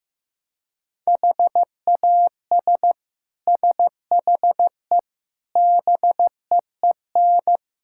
Morse code, heard as a keyed tone, about 15 wpm, 700 Hz. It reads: HAS SHE BEEN
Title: Sentences from Top 1000 Words - Review Part 6 - 15wpm